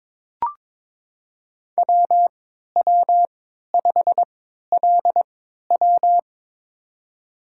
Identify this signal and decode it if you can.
Morse code, heard as a keyed tone, about 23 wpm, spaced as for 13 wpm, 700 Hz. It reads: WW5LW
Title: USA Call Signs - 5 Characters - 3x Character Spacing - 22wpm